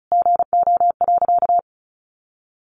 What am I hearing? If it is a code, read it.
Morse code, heard as a keyed tone, about 34 wpm, 700 Hz. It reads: GO.
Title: Sentences from Top 1400 Words 35wpm